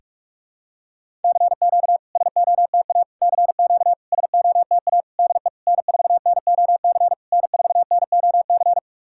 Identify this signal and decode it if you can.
Morse code, heard as a keyed tone, about 46 wpm, 690 Hz. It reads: CQ SOTA CQ SOTA DE N4NOC N4NOC
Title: SOTA QSO Practice 45wpm